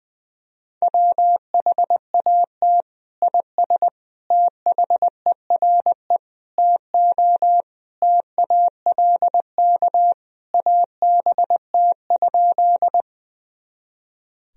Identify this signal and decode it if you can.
Morse code, heard as a keyed tone, about 20 wpm, 700 Hz. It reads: WHAT IS THERE TO TALK ABT?
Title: Sentences from Top 400 Words and Abbreviations 20wpm